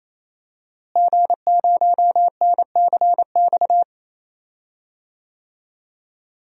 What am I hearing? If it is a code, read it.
Morse code, heard as a keyed tone, about 28 wpm, 700 Hz. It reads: G0NCX
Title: DXCC-100 Call Signs - 5 Characters - Repeated 3x - 28wpm